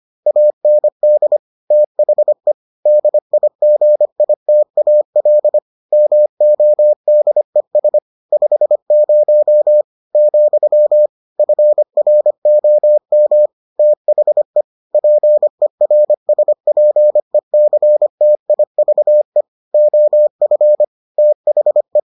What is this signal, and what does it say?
Morse code, heard as a keyed tone, about 25 wpm, 600 Hz. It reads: AND THE DIGITAL MODES 50 , FROM THE PERSPECTIVE OF THE